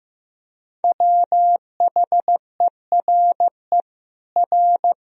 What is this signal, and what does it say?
Morse code, heard as a keyed tone, about 16 wpm, 700 Hz. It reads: WHERE R
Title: Sentences from Top 200 Words and Abbreviations 15wpm